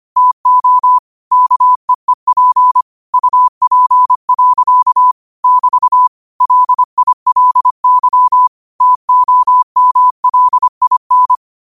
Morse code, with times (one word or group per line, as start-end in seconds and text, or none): 0.2-1.0 TO
1.3-2.8 KEEP
3.1-5.1 UP.
5.4-6.1 <BT>
6.4-8.5 LILY
8.8-11.3 TOMLIN